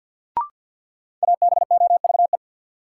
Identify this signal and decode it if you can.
Morse code, heard as a keyed tone, about 47 wpm, 700 Hz. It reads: ABOVE
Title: Sentences from Top 1000 Words - Review Part 2 - New Word Drill - 50wpm